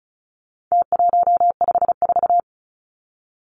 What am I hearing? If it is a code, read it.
Morse code, heard as a keyed tone, about 34 wpm, 700 Hz. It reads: T154
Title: Numbers - Contest Style - 4-Digit 35wpm